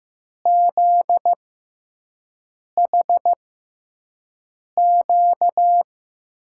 Morse code, heard as a keyed tone, about 15 wpm, spaced as for 5 wpm, 700 Hz. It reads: ZHQ
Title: USA Call Signs - 5 Characters - 6x Character Spacing - 15wpm